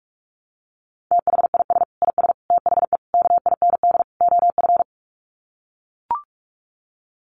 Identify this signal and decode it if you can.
Morse code, heard as a keyed tone, about 44 wpm, 700 Hz. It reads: THIS IS THE KIND OF
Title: Sets of 5 Words Encore 45wpm